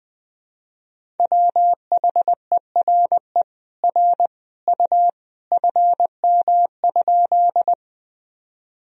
Morse code, heard as a keyed tone, about 20 wpm, 700 Hz. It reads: WHERE R U FM?
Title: Sentences from Top 200 Words and Abbreviations 20wpm